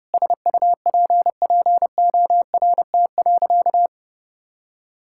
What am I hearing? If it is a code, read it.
Morse code, heard as a keyed tone, about 30 wpm, 700 Hz. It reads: SUPPORT.
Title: Sentences from Top 1000 Words - Review Part 5 - 30wpm